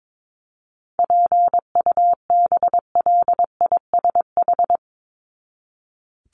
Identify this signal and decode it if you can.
Morse code, heard as a keyed tone, about 22 wpm, 700 Hz. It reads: PUBLISH
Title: ED Suffix - Top 100 Words with Base Words 22wpm